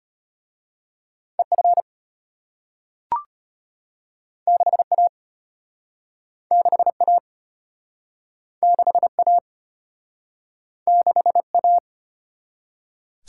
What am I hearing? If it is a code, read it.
Morse code, heard as a keyed tone, about 32 wpm, 700 Hz. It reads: EF 6A 6A 6A 6A